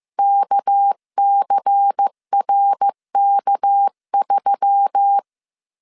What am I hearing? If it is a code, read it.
Morse code, heard as a keyed tone, about 15 wpm, 800 Hz. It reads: KCRK3
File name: CW / DET / 1 6 1500C S A 14wpm 2921